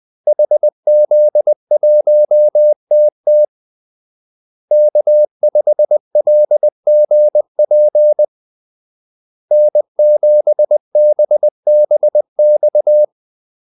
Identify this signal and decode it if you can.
Morse code, heard as a keyed tone, about 20 wpm, 600 Hz. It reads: HZ1TT K5LGP N7BBX
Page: Kaldesignaler 20 wpm